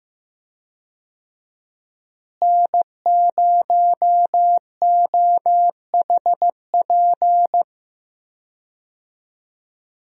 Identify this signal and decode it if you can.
Morse code, heard as a keyed tone, about 15 wpm, 700 Hz. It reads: N0OHP
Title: USA Call Signs - 5 Characters - Repeated 3x - 15wpm